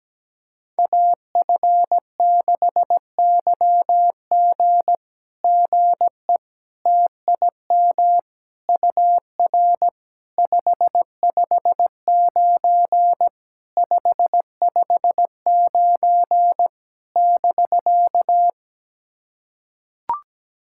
Morse code, heard as a keyed tone, about 17 wpm, 700 Hz. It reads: AF6YG GE TIM UR 559 559 <BK>
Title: SOTA QSO Practice 17wpm